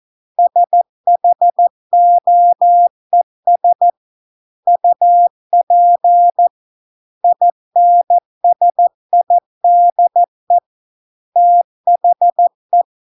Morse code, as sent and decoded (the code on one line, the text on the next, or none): ... .... --- . ... / ..- .--. / .. -. ... .. -.. / - .... .
SHOES UP INSIDE THE